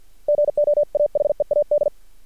Morse code, heard as a keyed tone, about 50 wpm, 590 Hz. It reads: GO AHEAD
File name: archived / CRSnet